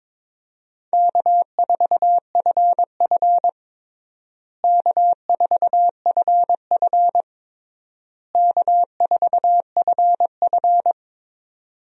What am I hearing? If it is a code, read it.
Morse code, heard as a keyed tone, about 22 wpm, 700 Hz. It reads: K4FF K4FF K4FF
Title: USA Call Signs - 4 Characters - Repeated 3x - 22wpm